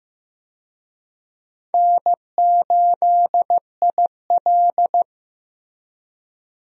Morse code, heard as a keyed tone, about 15 wpm, 700 Hz. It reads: N8IL
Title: USA Call Signs - 4 Characters - Repeated 2x - 15wpm